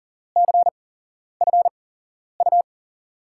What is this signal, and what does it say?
Morse code, heard as a keyed tone, about 40 wpm, spaced as for 11 wpm, 700 Hz. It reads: CFU
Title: USA Call Signs - 6 Characters - 8x Character Spacing - 40wpm